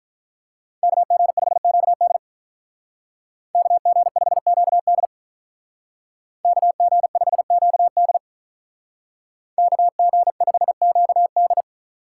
Morse code, heard as a keyed tone, about 42 wpm, 700 Hz. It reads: KG5QD KG5QD KG5QD KG5QD
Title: Speed Racing - USA 2x2 Call Signs 35wpm